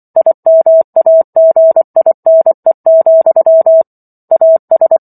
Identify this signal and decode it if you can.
Morse code, heard as a keyed tone, about 24 wpm, 650 Hz. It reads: IMAGINE, AS